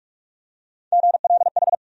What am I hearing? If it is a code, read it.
Morse code, heard as a keyed tone, about 45 wpm, 700 Hz. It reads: GLH
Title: USA Call Sign Postfixes - 3 Letters 45wpm